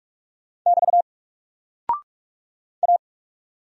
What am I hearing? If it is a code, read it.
Morse code, heard as a keyed tone, about 46 wpm, 700 Hz. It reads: <BT> A
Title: Single Character (Letter:Number:Punctuation:Prosign) 45wpm